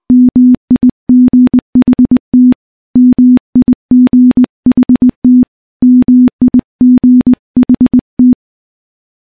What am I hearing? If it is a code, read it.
Morse code, heard as a keyed tone, about 21 wpm, 250 Hz. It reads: MIGHT MIGHT MIGHT
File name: morse-training-20WPM